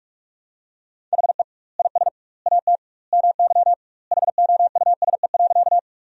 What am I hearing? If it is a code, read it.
Morse code, heard as a keyed tone, about 46 wpm, 700 Hz. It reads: HE IS AT MY HOUSE.
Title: 2x Word-Spacing - Sentences from Top 200 Words - 45wpm